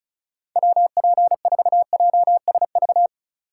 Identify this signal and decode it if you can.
Morse code, heard as a keyed tone, about 34 wpm, 700 Hz. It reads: WP4JSV
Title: USA Call Signs - 6 Characters - 2x Character Spacing - 35wpm